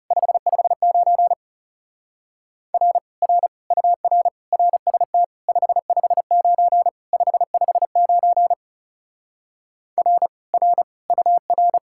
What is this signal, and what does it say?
Morse code, heard as a keyed tone, about 38 wpm, 700 Hz. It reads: 559 R R UR RST 559 559 R R UR